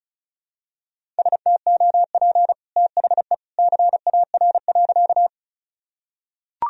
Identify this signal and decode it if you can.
Morse code, heard as a keyed tone, about 36 wpm, 690 Hz. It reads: STOP THE CAR.
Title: Sentences from Top 1000 Words - Review Part 1 - 35wpm